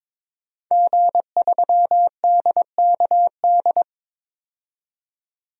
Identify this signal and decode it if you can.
Morse code, heard as a keyed tone, about 22 wpm, 700 Hz. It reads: G3DKD
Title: DXCC-100 Call Signs - 5 Characters - 22wpm